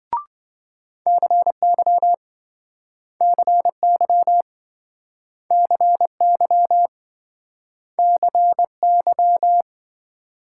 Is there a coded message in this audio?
Yes: Morse code from 1.1 to 9.6 s, heard as a keyed tone, about 27 wpm, 700 Hz. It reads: CY CY CY CY